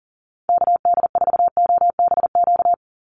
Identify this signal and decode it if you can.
Morse code, heard as a keyed tone, about 40 wpm, 700 Hz. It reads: KD4OBQ
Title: USA Call Signs 40wpm